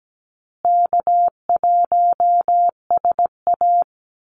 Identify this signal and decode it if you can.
Morse code, heard as a keyed tone, about 17 wpm, 700 Hz. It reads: K1SA
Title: Easy Call Signs Repeated 2x - 17wpm